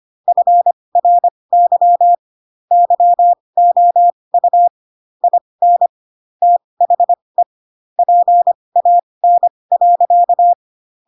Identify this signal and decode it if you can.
Morse code, heard as a keyed tone, about 25 wpm, spaced as for 20 wpm, 700 Hz. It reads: FRY YOU IN THE PAN.